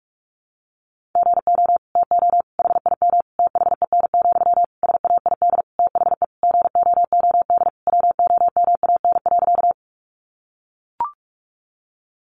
Morse code, heard as a keyed tone, about 44 wpm, 700 Hz. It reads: GO TO HIM THEN, SAID THE GOOD WOMAN.
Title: Sentences from Top 1100 Words 45wpm